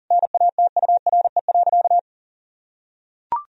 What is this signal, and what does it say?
Morse code, heard as a keyed tone, about 40 wpm, 700 Hz. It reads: NATURE.
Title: Sentences from Top 1000 Words - Review Part 3 - 40wpm